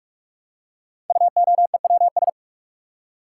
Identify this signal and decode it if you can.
Morse code, heard as a keyed tone, about 44 wpm, 700 Hz. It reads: UOEWS
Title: Code Groups - 5 Letters - 45wpm